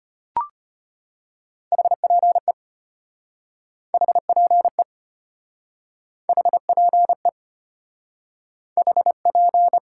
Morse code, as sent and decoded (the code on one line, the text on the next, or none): .... .--. / .... .--. / .... .--. / .... .--.
HPE HPE HPE HP